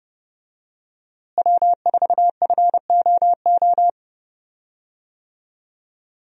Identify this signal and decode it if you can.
Morse code, heard as a keyed tone, about 30 wpm, 700 Hz. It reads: W4FOO
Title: USA Call Signs - 5 Characters - Repeated 3x - 30wpm